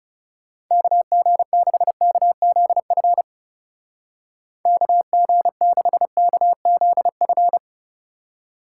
Morse code, heard as a keyed tone, about 36 wpm, 700 Hz. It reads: KG6KZF KG6KZF
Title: Speed Racing - USA Call Signs 30wpm